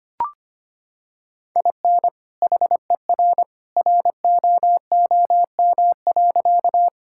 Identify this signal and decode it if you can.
Morse code, heard as a keyed tone, about 25 wpm, 700 Hz. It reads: IN HER ROOM.
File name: practice / 25wpm / Sentences with 2-Word Phrases - 25wpm